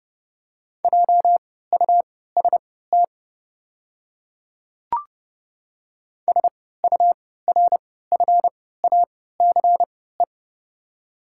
Morse code, heard as a keyed tone, about 31 wpm, spaced as for 17 wpm, 700 Hz. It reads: JUST SURFACE
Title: 3x Character Spacing - Top 500 Words - 30wpm